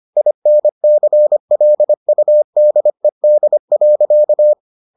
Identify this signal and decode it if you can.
Morse code, heard as a keyed tone, about 25 wpm, 600 Hz. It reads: INCLUDED.